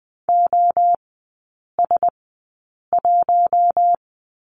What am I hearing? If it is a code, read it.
Morse code, heard as a keyed tone, about 20 wpm, spaced as for 8 wpm, 700 Hz. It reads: OS1
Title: 041 Lesson 8e - Number 1 - Three Characters 20wpm